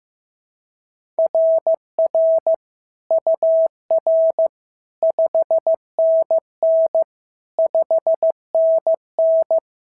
Morse code, heard as a keyed tone, about 14 wpm, 650 Hz. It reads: RR UR 5NN 5NN
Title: SOTA QSO Practice 15wpm